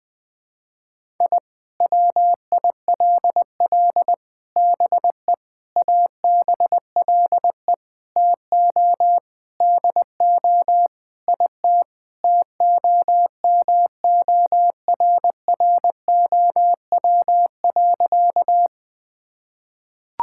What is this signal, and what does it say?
Morse code, heard as a keyed tone, about 20 wpm, 700 Hz. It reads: I WILL BE ABLE TO DO IT TOMORROW.
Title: Sentences with 4-Word Phrases - 20wpm